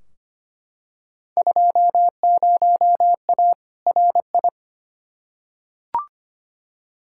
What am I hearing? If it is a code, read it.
Morse code, heard as a keyed tone, about 25 wpm, 700 Hz. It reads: 20A RI
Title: Field Day Exchanges Encore - 25wpm